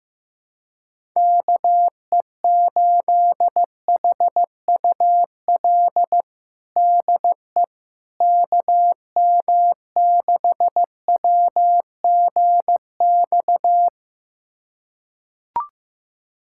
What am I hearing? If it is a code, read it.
Morse code, heard as a keyed tone, about 15 wpm, 700 Hz. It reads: KE8HUL DE KM6WGX
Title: USA Callsign DE Callsign 15wpm